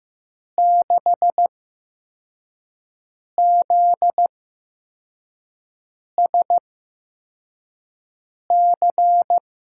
Morse code, heard as a keyed tone, about 15 wpm, spaced as for 4 wpm, 700 Hz. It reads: 6ZSC
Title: USA Call Signs - 6 Characters - 8x Character Spacing - 15wpm